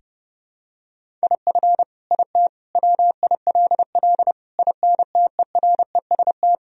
Morse code, heard as a keyed tone, about 32 wpm, 700 Hz. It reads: IF IT WILL INTEREST
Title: Sentences from Top 1000 Words - Review Part 2 - 30wpm